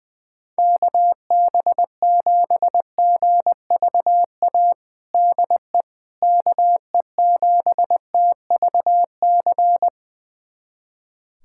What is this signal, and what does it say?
Morse code, heard as a keyed tone, about 20 wpm, 700 Hz. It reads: KB7GVA DE KE7TVC
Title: USA Callsign DE Callsign - 6 Characters - 20wpm